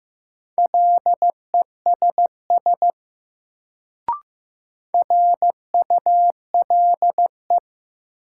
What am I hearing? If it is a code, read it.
Morse code, heard as a keyed tone, about 15 wpm, 700 Hz. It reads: LESS RULE
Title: Sentences from Top 400 Words - New Word Drill - 15wpm